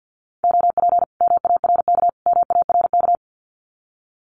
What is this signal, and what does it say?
Morse code, heard as a keyed tone, about 50 wpm, 700 Hz. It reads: OP MARK MARK
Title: Common QSO Phrases Encore 50wpm